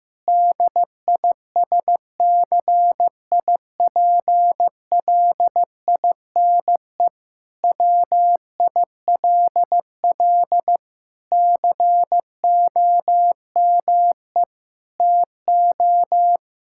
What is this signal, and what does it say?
Morse code, heard as a keyed tone, about 15 wpm, 700 Hz. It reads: DISCIPLINE WILL COME TO
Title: Sentences from Top 1000 Words - Review Part 6 - 15wpm